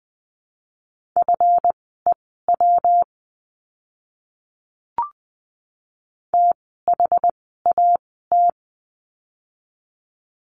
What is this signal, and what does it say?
Morse code, heard as a keyed tone, about 20 wpm, spaced as for 15 wpm, 700 Hz. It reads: FEW THAT
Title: 206 Lesson 33f - Letter j - Words 20wpm